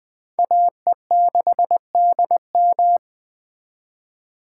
Morse code, heard as a keyed tone, about 20 wpm, 700 Hz. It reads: AE6DM
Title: Speed Racing - USA 2x2 Call Signs 17wpm